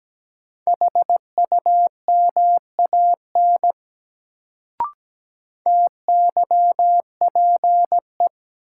Morse code, heard as a keyed tone, about 17 wpm, 700 Hz. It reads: HUMAN TYPE
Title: Sentences from Top 600 Words - New Word Drill - 17wpm